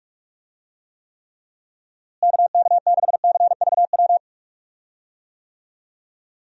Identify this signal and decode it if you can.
Morse code, heard as a keyed tone, about 41 wpm, 700 Hz. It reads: KK6CVW
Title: USA Call Signs - 6 Characters - Repeated 3x - 45wpm